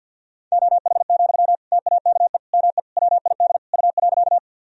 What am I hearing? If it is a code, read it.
Morse code, heard as a keyed tone, about 51 wpm, 700 Hz. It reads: OH, TAKE ME WID U.
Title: Sentences from Top 500 Words and Abbreviations 50wpm